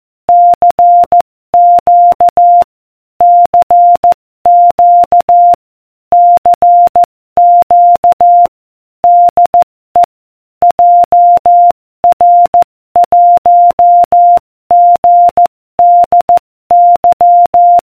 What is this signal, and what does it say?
Morse code, heard as a keyed tone, about 14 wpm, 700 Hz. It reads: CQ CQ CQ DE JR1GDY